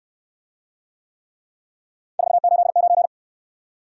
Morse code, heard as a keyed tone, about 60 wpm, 700 Hz. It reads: 489